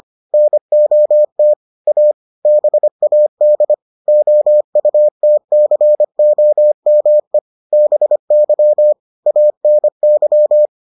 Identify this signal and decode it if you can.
Morse code, heard as a keyed tone, about 25 wpm, 600 Hz. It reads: NOT A BAD OUTCOME BY ANY